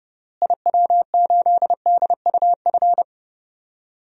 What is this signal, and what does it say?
Morse code, heard as a keyed tone, about 32 wpm, 700 Hz. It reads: IW8DUF
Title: DXCC-100 Call Signs - 6 Characters - Repeated 2x - 30wpm